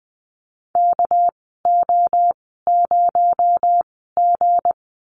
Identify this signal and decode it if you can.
Morse code, heard as a keyed tone, about 20 wpm, spaced as for 14 wpm, 700 Hz. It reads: KO0G
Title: 231 Lesson 37g - Letter x - Call Signs 20wpm